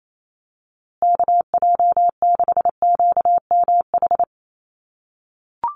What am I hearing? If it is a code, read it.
Morse code, heard as a keyed tone, about 28 wpm, 700 Hz. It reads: KJ6QMH
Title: USA Call Signs Repeated 2x 28wpm